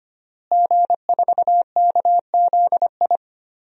Morse code, heard as a keyed tone, about 25 wpm, 700 Hz. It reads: G4KZI
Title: DXCC-100 Call Signs - 5 Characters - Repeated 2x - 25wpm